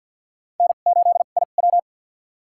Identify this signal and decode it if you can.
Morse code, heard as a keyed tone, about 51 wpm, spaced as for 36 wpm, 700 Hz. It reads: N8IW
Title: USA Call Signs - 4 Characters - 2x Character Spacing - 50wpm